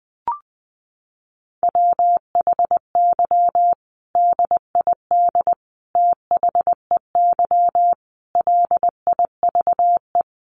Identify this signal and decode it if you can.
Morse code, heard as a keyed tone, about 20 wpm, 700 Hz. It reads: WHY DID THEY LIVE
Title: Sentences from Top 900 Words 20wpm